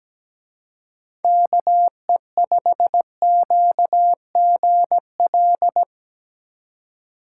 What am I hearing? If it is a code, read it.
Morse code, heard as a keyed tone, about 19 wpm, 700 Hz. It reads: KE5QGL
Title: Speed Racing - USA 2x3 Call Signs 17wpm